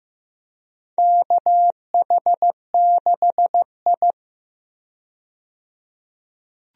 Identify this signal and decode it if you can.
Morse code, heard as a keyed tone, about 15 wpm, 700 Hz. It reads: KH6I
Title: DXCC-100 Call Signs - 4 Characters - 15wpm